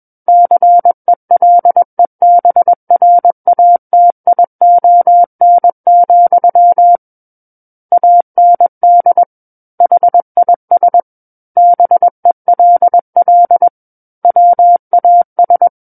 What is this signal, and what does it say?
Morse code, heard as a keyed tone, about 21 wpm, 700 Hz. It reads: CELEBRATION, AND HIS BELL WAS
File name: Book One / 12-Chapter 12 - 21 WPM